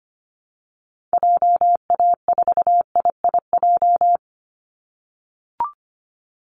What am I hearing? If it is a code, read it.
Morse code, heard as a keyed tone, about 25 wpm, 700 Hz. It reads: JA4IIJ
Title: Call Signs - 25wpm